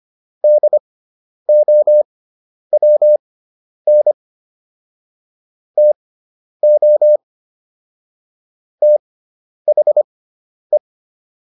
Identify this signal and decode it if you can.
Morse code, heard as a keyed tone, about 25 wpm, spaced as for 10 wpm, 600 Hz. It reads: DOWN TO THE